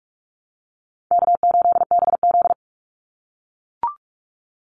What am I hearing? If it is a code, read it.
Morse code, heard as a keyed tone, about 44 wpm, 700 Hz. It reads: K8BZ
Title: Easy Call Signs - 45wpm